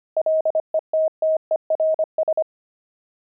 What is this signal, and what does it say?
Morse code, heard as a keyed tone, about 25 wpm, 630 Hz. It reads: LETTERS
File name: pangram-fs-25-cs-25